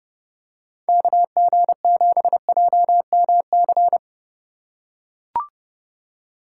Morse code, heard as a keyed tone, about 26 wpm, 700 Hz. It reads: KG7JMC